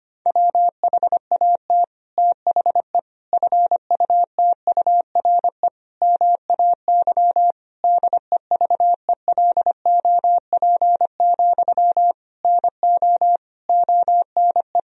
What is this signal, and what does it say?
Morse code, heard as a keyed tone, about 25 wpm, 700 Hz. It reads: WHAT THE FUTURE MAY DEVELOP, NO ONE
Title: Sentences from Top 1000 Words - Review Part 2 - 25wpm